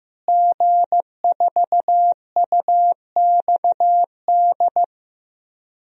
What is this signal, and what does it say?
Morse code, heard as a keyed tone, about 15 wpm, 700 Hz. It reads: G4UXD
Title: DXCC-100 Call Signs - 5 Characters - Repeated 2x - 15wpm